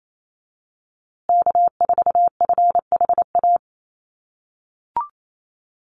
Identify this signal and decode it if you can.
Morse code, heard as a keyed tone, about 28 wpm, 700 Hz. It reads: K4FHA
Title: USA Call Signs Repeated 2x 28wpm